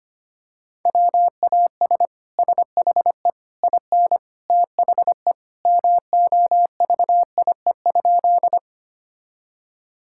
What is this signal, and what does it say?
Morse code, heard as a keyed tone, about 25 wpm, 700 Hz. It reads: WAS SHE IN THE MOVIE?
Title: Sentences from Top 1000 Words - Review Part 5 - 25wpm